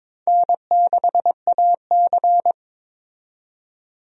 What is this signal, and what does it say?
Morse code, heard as a keyed tone, about 21 wpm, 700 Hz. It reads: N6AC
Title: USA Call Signs - 4 Characters - 22wpm